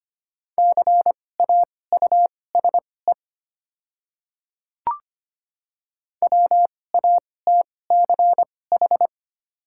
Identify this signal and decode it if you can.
Morse code, heard as a keyed tone, about 25 wpm, spaced as for 18 wpm, 700 Hz. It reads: CAUSE WATCH
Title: 2x Character Spacing - Top 300 Words - 25wpm